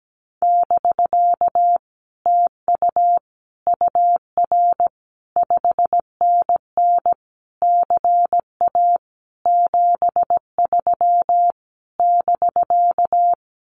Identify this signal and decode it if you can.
Morse code, heard as a keyed tone, about 17 wpm, 700 Hz. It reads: <BK> TU UR 5NN CA 73 <BK>
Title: Common QSO Phrases Encore 17wpm